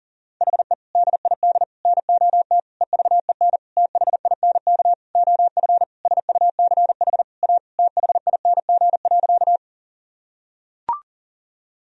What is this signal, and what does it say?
Morse code, heard as a keyed tone, about 40 wpm, 700 Hz. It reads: HE DID NOT EVEN THINK OF SUCH A THING.
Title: Sentences from Top 1000 Words - Review Part 1 - 40wpm